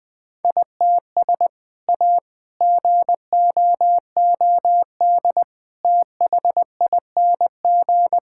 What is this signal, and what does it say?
Morse code, heard as a keyed tone, about 21 wpm, 700 Hz. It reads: ITS A GOOD THING
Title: Sentences from Top 1000 Words - Review Part 5 - 20wpm